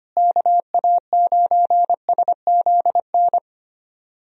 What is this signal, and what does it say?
Morse code, heard as a keyed tone, about 25 wpm, 700 Hz. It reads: KA9SZN
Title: USA Call Signs - 6 Characters - Repeated 2x - 25wpm